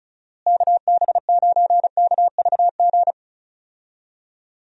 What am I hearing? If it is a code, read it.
Morse code, heard as a keyed tone, about 34 wpm, 700 Hz. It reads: KB9KVG
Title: USA Call Signs - 6 Characters - Repeated 3x - 35wpm